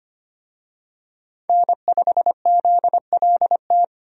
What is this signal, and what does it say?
Morse code, heard as a keyed tone, about 25 wpm, 700 Hz. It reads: N5ZLT
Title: USA Call Signs - 5 Characters - Repeated 2x - 25wpm